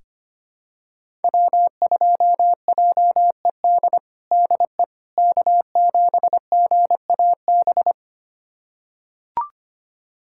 Morse code, heard as a keyed tone, about 25 wpm, 700 Hz. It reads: W2JED DE K7GAB
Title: USA Callsign DE Callsign - 5 Characters - 25wpm